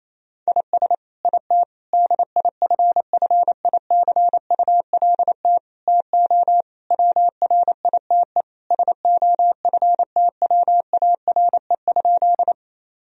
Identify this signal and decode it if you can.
Morse code, heard as a keyed tone, about 28 wpm, 700 Hz. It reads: IS IT DIFFICULT TO WRITE SOFTWARE?
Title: Sentences from Top 1000 Words - Review Part 3 - 28wpm